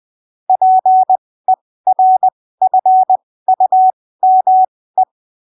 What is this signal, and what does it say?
Morse code, heard as a keyed tone, about 20 wpm, spaced as for 15 wpm, 750 Hz. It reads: PERFUME
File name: arquivos / texto / texto 20 20ppm